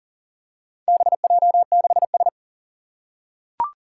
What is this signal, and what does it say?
Morse code, heard as a keyed tone, about 38 wpm, 700 Hz. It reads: BJ6S
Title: Four Challenging Letter-Number Combinations 40wpm